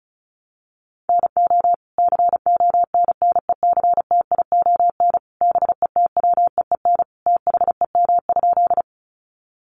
Morse code, heard as a keyed tone, about 34 wpm, 700 Hz. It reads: NO CONNECTION BETWEEN THEM?
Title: Sentences from Top 1300 Words 35wpm